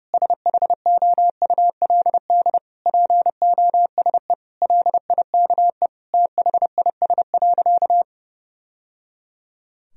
Morse code, heard as a keyed tone, about 29 wpm, 700 Hz. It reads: SHOULD POSE LIKE THIS.
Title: Sentences from Top 1000 Words - Review Part 2 - 30wpm